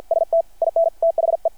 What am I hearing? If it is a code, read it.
Morse code, heard as a keyed tone, about 53 wpm, 660 Hz. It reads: ST IN THE
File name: archived / CRSnet